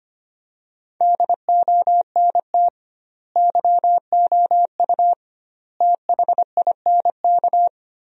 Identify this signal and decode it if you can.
Morse code, heard as a keyed tone, about 25 wpm, 700 Hz. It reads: DONT YOU THINK
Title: 2x Word-Spacing - Sentences from Top 300 Words - 25wpm